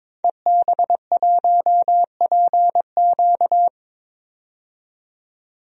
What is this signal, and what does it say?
Morse code, heard as a keyed tone, about 22 wpm, 700 Hz. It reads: EB1PQ
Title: DXCC-100 Call Signs - 5 Characters - 22wpm